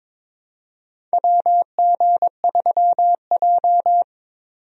Morse code, heard as a keyed tone, about 22 wpm, 700 Hz. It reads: WG3J
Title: USA Call Signs - 4 Characters - 22wpm